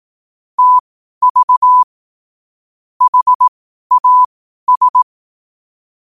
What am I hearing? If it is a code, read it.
Morse code, heard as a keyed tone, about 18 wpm, spaced as for 12 wpm, 1000 Hz. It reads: TV HAS